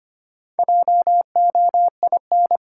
Morse code, heard as a keyed tone, about 25 wpm, 700 Hz. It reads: JOIN
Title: Sentences from Top 1000 Words - Review Part 6 - New Word Drill - 25wpm